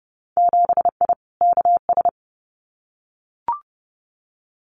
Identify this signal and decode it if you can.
Morse code, heard as a keyed tone, about 30 wpm, 700 Hz. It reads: 7I KS
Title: Winter Field Day Exchanges Encore 30wpm